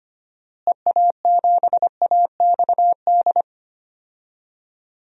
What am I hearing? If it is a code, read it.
Morse code, heard as a keyed tone, about 25 wpm, 700 Hz. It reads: EA7AXD
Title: DXCC-100 Call Signs - 6 Characters - Repeated 3x - 25wpm